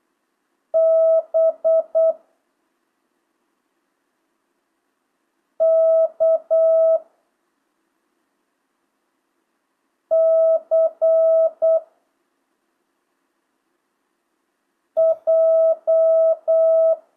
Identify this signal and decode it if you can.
Morse code, heard as a keyed tone, about 8 wpm, 650 Hz. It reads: B K C J
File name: jh8cbh / 2025cw